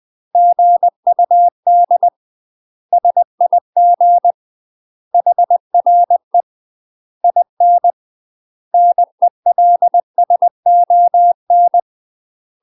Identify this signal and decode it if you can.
Morse code, heard as a keyed tone, about 20 wpm, 700 Hz. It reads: GUD SIG HRE IN NELSON
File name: 20WPM